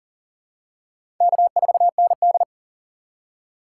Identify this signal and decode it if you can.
Morse code, heard as a keyed tone, about 40 wpm, 700 Hz. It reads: K4ND
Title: USA Call Signs - 2x Character Spacing - 40wpm